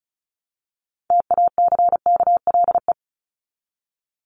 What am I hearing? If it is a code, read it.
Morse code, heard as a keyed tone, about 34 wpm, 700 Hz. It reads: TACKLE
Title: Top 1751-2000 Words 35wpm